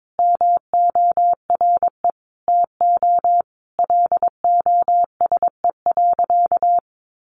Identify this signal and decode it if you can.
Morse code, heard as a keyed tone, about 22 wpm, 700 Hz. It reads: MORE TO LOSE.
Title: Sentences from Top 1100 Words 22wpm